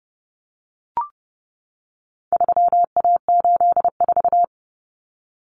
Morse code, heard as a keyed tone, about 29 wpm, 700 Hz. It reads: 3A84
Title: Common Cut Numbers - 4-Digit 30wpm